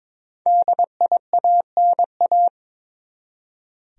Morse code, heard as a keyed tone, about 22 wpm, 700 Hz. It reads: DIANA
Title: USA Common Names - Max 5 Letters - 22wpm